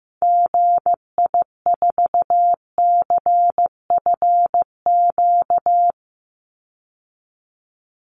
Morse code, heard as a keyed tone, about 15 wpm, 700 Hz. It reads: GI4CFQ
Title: Call Signs Repeated 3x - 15wpm